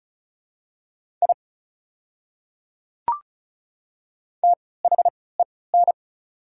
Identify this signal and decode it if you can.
Morse code, heard as a keyed tone, about 34 wpm, spaced as for 20 wpm, 690 Hz. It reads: I THEN
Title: 3x Character Spacing - Top 100 Words - 35wpm